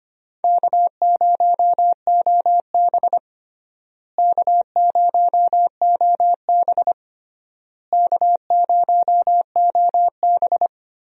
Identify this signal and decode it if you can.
Morse code, heard as a keyed tone, about 25 wpm, 700 Hz. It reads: K0OB K0OB K0OB
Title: USA Call Signs - 4 Characters - Repeated 3x - 25wpm